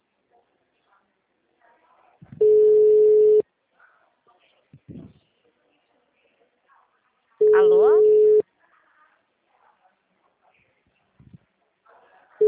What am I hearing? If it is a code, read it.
Ringback tone (Europe, 425 Hz), with speech.